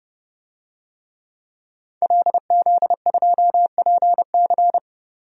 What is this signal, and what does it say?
Morse code, heard as a keyed tone, about 31 wpm, 700 Hz. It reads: LZ2PC
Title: DXCC-100 Call Signs - 5 Characters - 30wpm